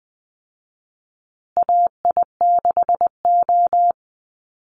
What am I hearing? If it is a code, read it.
Morse code, heard as a keyed tone, about 20 wpm, 700 Hz. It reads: AI6O
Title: Call Signs Repeated 3x - 20wpm